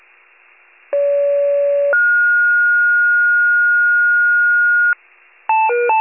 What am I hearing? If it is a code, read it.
Two-tone page, heard as a one-second tone, then a three-second tone, about 570 then 1430 Hz. It is heard from 0.9 to 4.9 s.
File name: wcalert